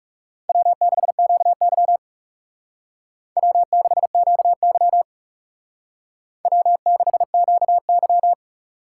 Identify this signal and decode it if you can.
Morse code, heard as a keyed tone, about 35 wpm, 700 Hz. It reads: W6QY W6QY W6QY
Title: Speed Racing - USA 1x2 Call Signs 30wpm